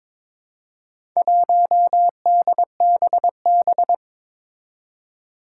Morse code, heard as a keyed tone, about 22 wpm, 700 Hz. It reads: 1DBB
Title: Four Challenging Letter-Number Combinations 22wpm